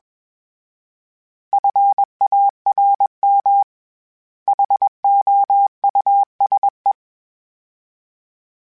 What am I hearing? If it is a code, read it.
Morse code, heard as a keyed tone, about 21 wpm, 800 Hz. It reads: FARM HOUSE
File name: General / General-mp3s / Houses @20